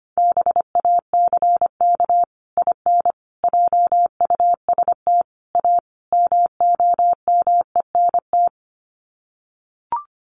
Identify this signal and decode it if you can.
Morse code, heard as a keyed tone, about 25 wpm, 700 Hz. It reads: BACK IN JUST A MOMENT
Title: Sets of 5 Words Encore 25wpm